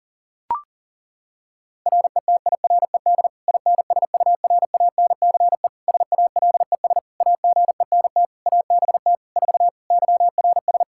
Morse code, heard as a keyed tone, about 40 wpm, 700 Hz. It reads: RETIRED INSURANCE SALES AGENT ABT 4 YRS
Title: K1USN SST QSO Practice 40wpm